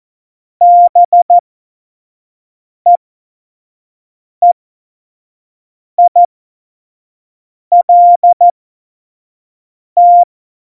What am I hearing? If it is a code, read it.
Morse code, heard as a keyed tone, about 14 wpm, spaced as for 5 wpm, 700 Hz. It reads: BEEILT
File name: Morsekurs / Witze / Witz 19. Juni 2018, 25 F70